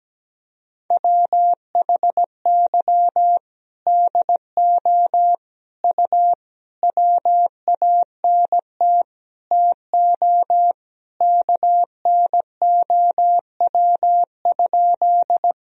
Morse code, heard as a keyed tone, about 17 wpm, 700 Hz. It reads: WHY DO U WANT TO KNOW?
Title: Sentences from Top 200 Words and Abbreviations 17wpm